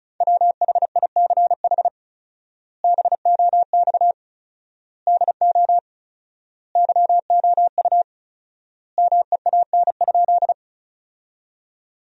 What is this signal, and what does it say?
Morse code, heard as a keyed tone, about 36 wpm, 700 Hz. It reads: WHICH BOX DO YOU MEAN?
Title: 4x Word-Spacing - Sentences from Top 500 Words - 35wpm